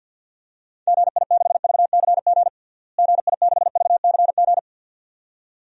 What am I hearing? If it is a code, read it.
Morse code, heard as a keyed tone, about 52 wpm, 700 Hz. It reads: GI6VCG GI6VCG
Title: DXCC-100 Call Signs - 6 Characters - Repeated 2x - 50wpm